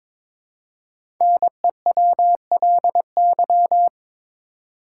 Morse code, heard as a keyed tone, about 22 wpm, 690 Hz. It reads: NEWLY
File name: 5 Letter Words - First 3 Letters Given - 22wpm